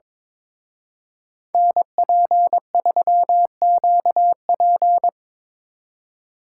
Morse code, heard as a keyed tone, about 27 wpm, 700 Hz. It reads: NP3QP
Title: Speed Racing - USA 2x2 Call Signs 22wpm